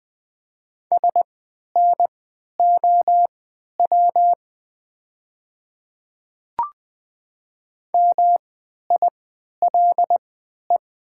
Morse code, heard as a keyed tone, about 20 wpm, spaced as for 11 wpm, 700 Hz. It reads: SNOW MILE
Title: 3x Character Spacing - Top 500 Words - 20wpm